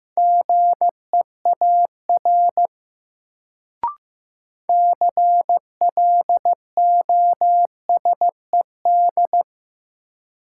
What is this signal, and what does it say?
Morse code, heard as a keyed tone, about 16 wpm, 700 Hz. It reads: GEAR CLOSED
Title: Sentences from Top 1400 Words - New Word Drill - 15wpm